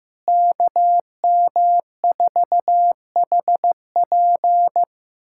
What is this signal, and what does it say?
Morse code, heard as a keyed tone, about 16 wpm, 700 Hz. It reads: KM4HP
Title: USA Callsign DE Callsign 15wpm